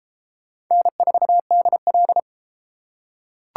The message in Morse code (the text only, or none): N4DL